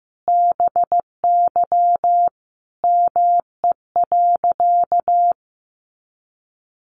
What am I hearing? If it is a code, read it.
Morse code, heard as a keyed tone, about 14 wpm, 700 Hz. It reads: BY ME.
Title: Sentences from Top 500 Words 15wpm